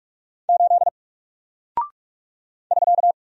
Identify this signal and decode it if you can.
Morse code, heard as a keyed tone, about 46 wpm, 700 Hz. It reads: 8 <SK>
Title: Single Character (Letter:Number:Punctuation:Prosign) Alt 45wpm